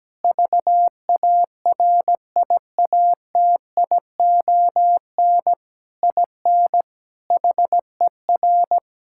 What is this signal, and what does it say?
Morse code, heard as a keyed tone, about 17 wpm, 700 Hz. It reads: VARIATION IN HER
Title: Sentences from Top 1600 Words 17wpm